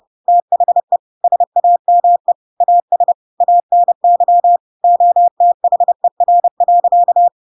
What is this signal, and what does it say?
Morse code, heard as a keyed tone, about 30 wpm, 700 Hz. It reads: THE SAME AS ANY OTHER.
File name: Pinocchio / Pinocchio ch1 30 30 ik1hge